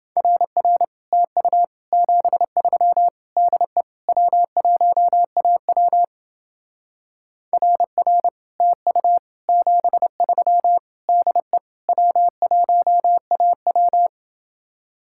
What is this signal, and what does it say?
Morse code, heard as a keyed tone, about 29 wpm, 700 Hz. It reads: RR TU 73 DE W1AW RR TU 73 DE W1AW